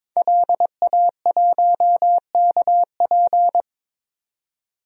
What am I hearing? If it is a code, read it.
Morse code, heard as a keyed tone, about 21 wpm, 700 Hz. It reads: LA1KP
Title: DXCC-100 Call Signs - 5 Characters - 22wpm